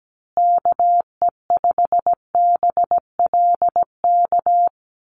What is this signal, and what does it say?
Morse code, heard as a keyed tone, about 17 wpm, 700 Hz. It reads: KE5BLK
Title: USA Call Signs Repeated 2x 17wpm